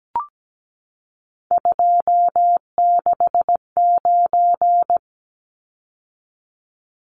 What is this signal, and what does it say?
Morse code, heard as a keyed tone, about 17 wpm, 700 Hz. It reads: 269
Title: Numbers - 3-Digit 17wpm